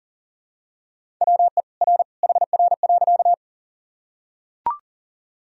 Morse code, heard as a keyed tone, about 40 wpm, 700 Hz. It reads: WE R HR.
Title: Sentences from Top 200 Words and Abbreviations 40wpm